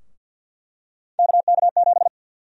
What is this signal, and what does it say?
Morse code, heard as a keyed tone, about 48 wpm, 700 Hz. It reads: KK7
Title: USA Call Sign Prefixes - Repeated 2x - 50wpm